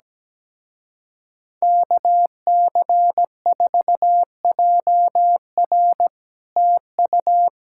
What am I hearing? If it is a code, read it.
Morse code, heard as a keyed tone, about 17 wpm, 700 Hz. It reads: KC4JR TU
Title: Speed-Racing Common QSO Phrases - 17wpm